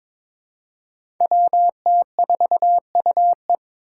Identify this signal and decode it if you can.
Morse code, heard as a keyed tone, about 22 wpm, 700 Hz. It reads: WT4UE
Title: Five Challenging Letter-Number Combinations 22wpm